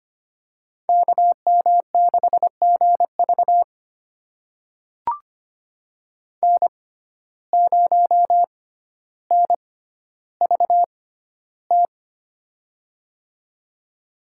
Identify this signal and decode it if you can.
Morse code, heard as a keyed tone, about 25 wpm, spaced as for 13 wpm, 700 Hz. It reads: KM6GV N0NVT
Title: USA Call Signs - 5 Characters - 6x Character Spacing - 25wpm